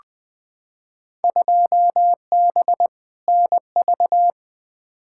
Morse code, heard as a keyed tone, about 20 wpm, 700 Hz. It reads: 2B NV
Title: Field Day Exchanges Encore - 20wpm